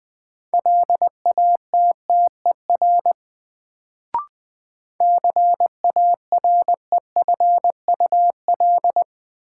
Morse code, heard as a keyed tone, about 20 wpm, 700 Hz. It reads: LATTER CAREFUL
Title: Sentences from Top 1600 Words - New Word Drill - 20wpm